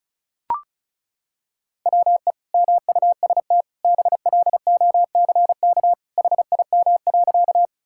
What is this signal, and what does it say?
Morse code, heard as a keyed tone, about 36 wpm, 700 Hz. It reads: WE MUST BLOCK HIM.
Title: Sentences from Top 1600 Words 35wpm